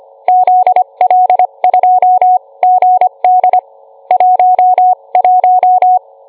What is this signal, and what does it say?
Morse code, heard as a keyed tone, about 25 wpm, spaced as for 22 wpm, 730 Hz. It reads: ZL2GD 11